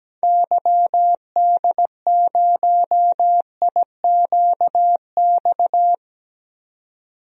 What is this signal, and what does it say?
Morse code, heard as a keyed tone, about 17 wpm, 700 Hz. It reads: YD0IQX
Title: DXCC-100 Call Signs - 6 Characters - 17wpm